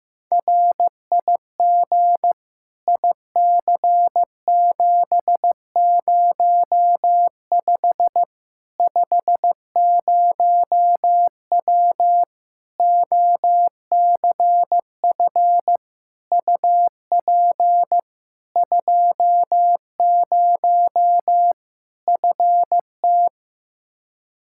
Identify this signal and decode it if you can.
Morse code, heard as a keyed tone, about 15 wpm, 700 Hz. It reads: RIG IC705 50W OCF UP 20 FT
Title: Practice QSO - Exchange 2 - Info - 15wpm